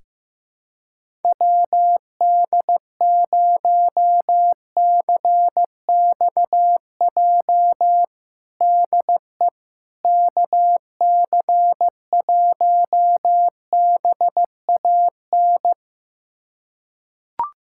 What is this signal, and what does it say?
Morse code, heard as a keyed tone, about 15 wpm, 700 Hz. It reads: WD0CXJ DE KC1BAN
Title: USA Callsign DE Callsign - 6 Characters - 15wpm